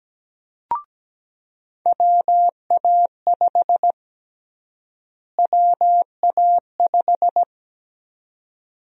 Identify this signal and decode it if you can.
Morse code, heard as a keyed tone, about 17 wpm, 700 Hz. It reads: WA5 WA5
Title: USA Call Sign Prefixes - Repeated 2x - 17wpm